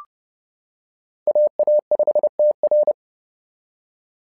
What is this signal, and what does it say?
Morse code, heard as a keyed tone, about 30 wpm, 600 Hz. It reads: AA5TR
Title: SOTA QSO Practice 30wpm